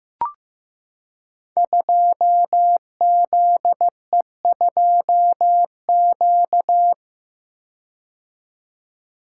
Morse code, heard as a keyed tone, about 15 wpm, 700 Hz. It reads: 2ZE2Q
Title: Code Groups - 5 Letter:Numbers - 15wpm